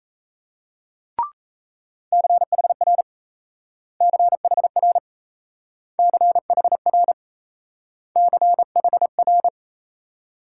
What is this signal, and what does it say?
Morse code, heard as a keyed tone, about 36 wpm, 700 Hz. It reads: CHR CHR CHR CHR